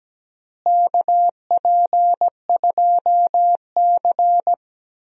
Morse code, heard as a keyed tone, about 17 wpm, 700 Hz. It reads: KP2C
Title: DXCC-100 Call Signs - 4 Characters - Repeated 3x - 17wpm